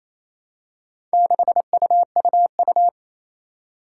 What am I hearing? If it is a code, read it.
Morse code, heard as a keyed tone, about 28 wpm, 700 Hz. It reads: 6UUU
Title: Four Challenging Letter-Number Combinations 28wpm